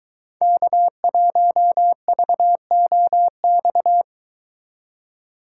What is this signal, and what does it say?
Morse code, heard as a keyed tone, about 21 wpm, 700 Hz. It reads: K1VOX